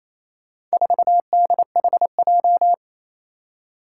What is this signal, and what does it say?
Morse code, heard as a keyed tone, about 28 wpm, 700 Hz. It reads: 4DHJ
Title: Four Challenging Letter-Number Combinations 28wpm